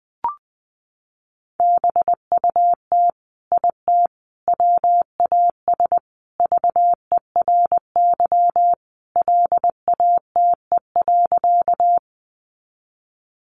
Morse code, heard as a keyed tone, about 20 wpm, 700 Hz. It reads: BUT IT WAS VERY LATE.
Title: Sentences from Top 300 Words 20wpm